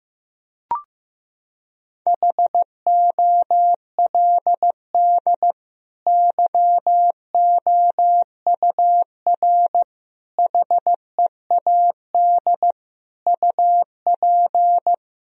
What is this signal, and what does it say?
Morse code, heard as a keyed tone, about 15 wpm, 700 Hz. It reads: HOLD YOUR HEAD UP
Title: Sentences from Top 1000 Words - Review Part 2 - 15wpm